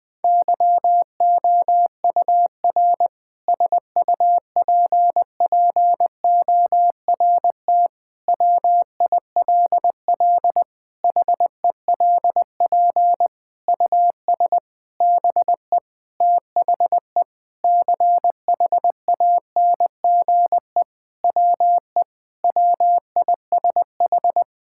Morse code, heard as a keyed tone, about 20 wpm, 700 Hz. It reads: YOUR SUPPORT WILL HELP US BE THE CHANGE WE WISH
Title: Sentences with 7-Word Phrases - 20wpm